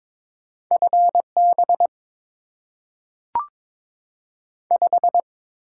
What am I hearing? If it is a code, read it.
Morse code, heard as a keyed tone, about 22 wpm, 700 Hz. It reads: FB 5
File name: Top 90 QSO Elements - 6x Character Spacing - 22wpm